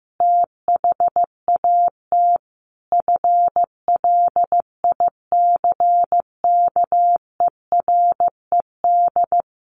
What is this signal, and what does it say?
Morse code, heard as a keyed tone, about 15 wpm, 700 Hz. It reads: THAT FLICKERED
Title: A Sound Of Thunder 15wpm